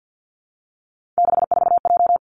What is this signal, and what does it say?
Morse code, heard as a keyed tone, about 50 wpm, 700 Hz. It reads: 64J
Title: Three Challenging Letter-Number Combinations 50wpm